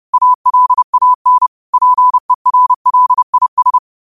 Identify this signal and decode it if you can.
Morse code, heard as a keyed tone, about 30 wpm, 1000 Hz. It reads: ALAN PERLIS
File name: 30wpm